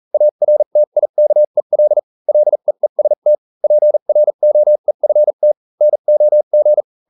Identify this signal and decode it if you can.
Morse code, heard as a keyed tone, about 40 wpm, spaced as for 35 wpm, 600 Hz. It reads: ARTIKEL LEEST PROEFT NOG